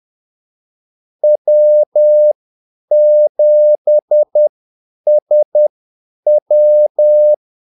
Morse code, heard as a keyed tone, about 10 wpm, spaced as for 8 wpm, 600 Hz. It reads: W7SW
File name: Callsigns-10-8wpm